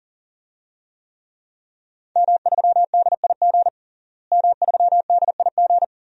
Morse code, heard as a keyed tone, about 40 wpm, 700 Hz. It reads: M3DIG M3DIG
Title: DXCC-100 Call Signs - 5 Characters - Repeated 3x - 40wpm